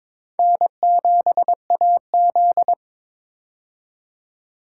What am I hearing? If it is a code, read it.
Morse code, heard as a keyed tone, about 21 wpm, 700 Hz. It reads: N7AZ
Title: K1USN SST QSO Practice - 22wpm